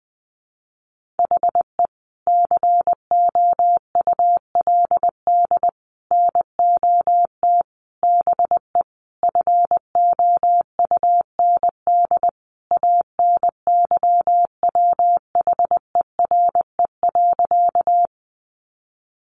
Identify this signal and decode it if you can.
Morse code, heard as a keyed tone, about 20 wpm, 700 Hz. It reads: HE COULD NOT BE FOUND ANYWHERE.
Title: Sentences from Top 1400 Words 20wpm